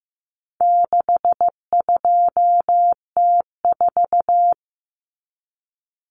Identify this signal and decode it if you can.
Morse code, heard as a keyed tone, about 15 wpm, 700 Hz. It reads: 62T4
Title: Common Cut Numbers - 4-Digit 15wpm